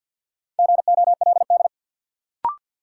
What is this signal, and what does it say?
Morse code, heard as a keyed tone, about 50 wpm, 700 Hz. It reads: GOLD
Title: Sentences from Top 1000 Words - Review Part 3 - New Word Drill - 50wpm